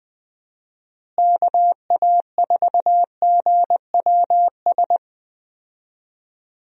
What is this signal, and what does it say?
Morse code, heard as a keyed tone, about 20 wpm, 700 Hz. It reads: KA4GWS
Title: USA Call Signs - 6 Characters - Repeated 2x - 20wpm